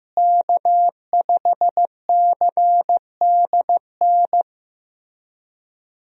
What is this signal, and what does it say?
Morse code, heard as a keyed tone, about 15 wpm, 700 Hz. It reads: K5CDN
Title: USA Call Signs - 5 Characters - Repeated 3x - 15wpm